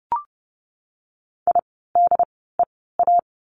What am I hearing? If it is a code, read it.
Morse code, heard as a keyed tone, about 30 wpm, spaced as for 17 wpm, 700 Hz. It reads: IDEA
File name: course / 30:10wpm / 108 Lesson 18f - Letter m - Words 30wpm